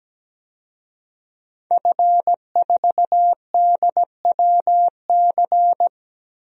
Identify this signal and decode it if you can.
Morse code, heard as a keyed tone, about 17 wpm, 700 Hz. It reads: F4DWC
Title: DXCC-100 Call Signs - 5 Characters - Repeated 2x - 17wpm